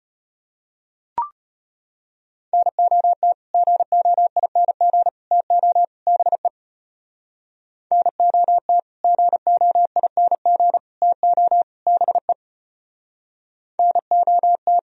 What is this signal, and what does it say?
Morse code, heard as a keyed tone, about 33 wpm, 700 Hz. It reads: NOT GOING TO BE NOT GOING TO BE NOT